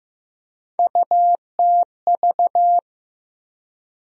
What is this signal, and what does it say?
Morse code, heard as a keyed tone, about 15 wpm, 700 Hz. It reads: UTV
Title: USA Call Sign Postfixes - 3 Letters 15wpm